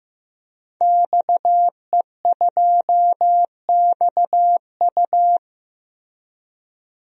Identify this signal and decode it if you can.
Morse code, heard as a keyed tone, about 15 wpm, 700 Hz. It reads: XE2XU
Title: DXCC-100 Call Signs - 5 Characters - 15wpm